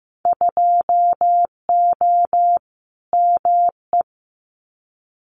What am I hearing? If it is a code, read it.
Morse code, heard as a keyed tone, about 15 wpm, 700 Hz. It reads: 2O ME
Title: Winter Field Day Exchanges Encore 15wpm